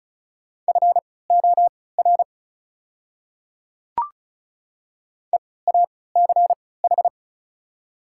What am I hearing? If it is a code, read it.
Morse code, heard as a keyed tone, about 36 wpm, spaced as for 20 wpm, 700 Hz. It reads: FOR EACH
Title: 3x Character Spacing - Top 100 Words - 35wpm